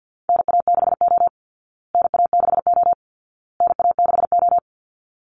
Morse code, heard as a keyed tone, about 50 wpm, 700 Hz. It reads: NA6O NA6O NA6O
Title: Call Signs Repeated 2x - 50wpm